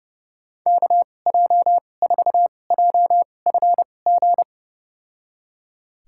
Morse code, heard as a keyed tone, about 30 wpm, spaced as for 22 wpm, 700 Hz. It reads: KJ4JFG
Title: USA Call Signs - 6 Characters - 2x Character Spacing - 30wpm